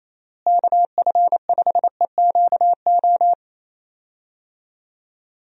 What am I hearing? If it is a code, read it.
Morse code, heard as a keyed tone, about 27 wpm, 700 Hz. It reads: KF5EQO
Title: USA Call Signs - 6 Characters - Repeated 2x - 28wpm